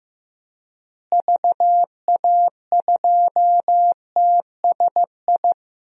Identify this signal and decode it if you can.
Morse code, heard as a keyed tone, about 15 wpm, 700 Hz. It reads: VA2TSI
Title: DXCC-100 Call Signs - 6 Characters - Repeated 2x - 15wpm